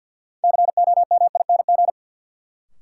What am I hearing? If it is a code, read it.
Morse code, heard as a keyed tone, about 53 wpm, 700 Hz. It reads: COMING
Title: Sentences from Top 1300 Words - New Word Drill Alt - 50wpm